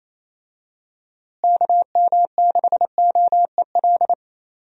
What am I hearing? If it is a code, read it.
Morse code, heard as a keyed tone, about 28 wpm, 700 Hz. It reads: KM6OEL
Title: USA Call Signs - 6 Characters - Repeated 3x - 28wpm